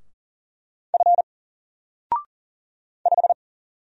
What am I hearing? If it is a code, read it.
Morse code, heard as a keyed tone, about 40 wpm, 700 Hz. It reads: F 5
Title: Single Character (Letter:Number:Punctuation:Prosign) 40wpm